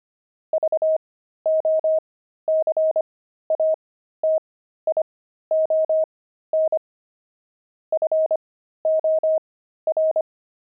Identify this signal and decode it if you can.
Morse code, heard as a keyed tone, about 25 wpm, spaced as for 13 wpm, 630 Hz. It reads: VOCATION FOR
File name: pangram-fs-13-cs-25-friars